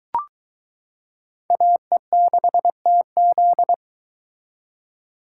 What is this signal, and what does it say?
Morse code, heard as a keyed tone, about 22 wpm, 700 Hz. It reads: AE6TZ